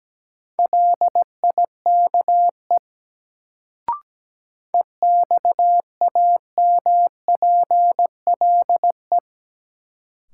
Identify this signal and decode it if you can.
Morse code, heard as a keyed tone, about 17 wpm, 700 Hz. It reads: LIKE EXAMPLE
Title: Sets of 2 Words - New Word Drill - 17wpm